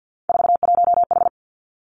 Morse code, heard as a keyed tone, about 50 wpm, 700 Hz. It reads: 41H
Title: Three Challenging Letter-Number Combinations 50wpm